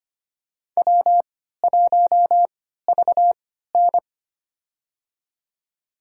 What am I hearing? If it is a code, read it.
Morse code, heard as a keyed tone, about 25 wpm, spaced as for 14 wpm, 700 Hz. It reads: W1VN
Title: USA Call Signs - 3x Character Spacing - 25wpm